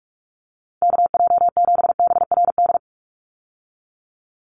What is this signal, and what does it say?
Morse code, heard as a keyed tone, about 44 wpm, 700 Hz. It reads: KJ7BRD
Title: USA Call Signs 45wpm